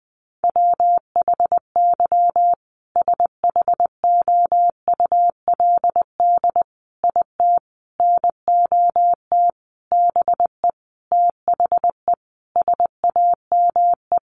Morse code, heard as a keyed tone, about 20 wpm, 700 Hz. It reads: WHY SHOULD IT NOT BE THE SAME
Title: Sentences from Top 1400 Words 20wpm